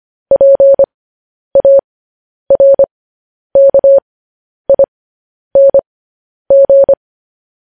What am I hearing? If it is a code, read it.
Morse code, heard as a keyed tone, about 25 wpm, spaced as for 10 wpm, 550 Hz. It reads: PARKING
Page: news headlines in morse code at 10 wpm